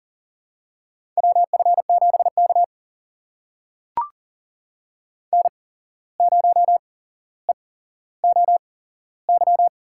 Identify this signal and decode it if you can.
Morse code, heard as a keyed tone, about 40 wpm, spaced as for 16 wpm, 700 Hz. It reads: WF7K N0EOY
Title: USA Call Signs - 8x Character Spacing - 40wpm